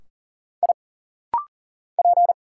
Morse code, heard as a keyed tone, about 40 wpm, 700 Hz. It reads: I P
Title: Single Letter-Number Alt - Rapid-Fire 40wpm